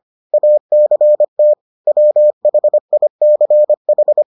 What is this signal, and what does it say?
Morse code, heard as a keyed tone, about 25 wpm, 600 Hz. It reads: ACT WHICH